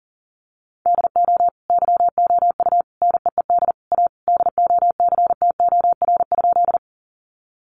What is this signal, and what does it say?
Morse code, heard as a keyed tone, about 40 wpm, 700 Hz. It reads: DO YOU NEED A DOCTOR?
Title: Sentences from Top 1500 Words 40wpm